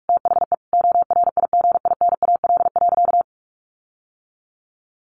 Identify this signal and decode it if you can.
Morse code, heard as a keyed tone, about 44 wpm, 700 Hz. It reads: THE ORIGINAL.
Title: Sentences from Top 1500 Words 45wpm